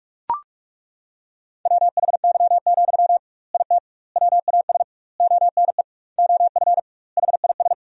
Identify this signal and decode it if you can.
Morse code, heard as a keyed tone, about 42 wpm, 700 Hz. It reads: WHY, IT WAS ONE OF HIS
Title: 2x Word-Spacing - Sentences from Top 300 Words - 45wpm